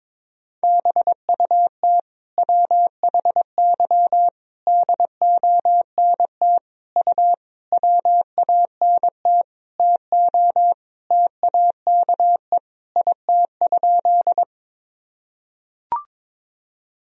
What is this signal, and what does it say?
Morse code, heard as a keyed tone, about 22 wpm, 700 Hz. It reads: BUT WHY DONT U WANT TO TAKE IT?
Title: Sentences from Top 300 Words and Abbreviations 22wpm